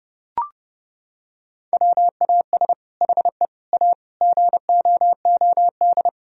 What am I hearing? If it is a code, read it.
Morse code, heard as a keyed tone, about 30 wpm, 700 Hz. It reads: WAS HE A GOOD
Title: Sentences from Top 1000 Words - Review Part 4 - 30wpm